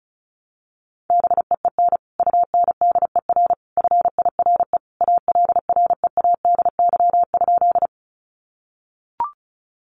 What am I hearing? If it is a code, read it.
Morse code, heard as a keyed tone, about 34 wpm, 700 Hz. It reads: BEEN UNDER FIRE ALREADY?
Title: Sentences from Top 700 Words 35wpm